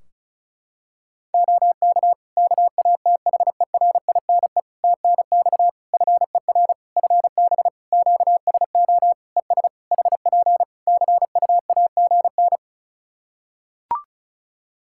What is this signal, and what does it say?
Morse code, heard as a keyed tone, about 34 wpm, 700 Hz. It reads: OK KATHERINE TNX FER FB QSO ES HP CUAGN
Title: K1USN SST QSO Practice 35wpm